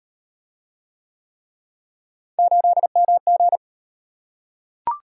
Morse code, heard as a keyed tone, about 38 wpm, 700 Hz. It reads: 8MG